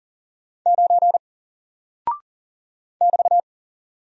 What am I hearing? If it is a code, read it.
Morse code, heard as a keyed tone, about 40 wpm, 700 Hz. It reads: 9 <BT>
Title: Single Character (Letter:Number:Punctuation:Prosign) 40wpm